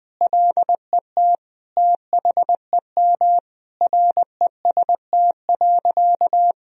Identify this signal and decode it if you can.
Morse code, heard as a keyed tone, about 21 wpm, 700 Hz. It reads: LET THEM REST.
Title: Sentences from Top 1000 Words - Review Part 4 - 20wpm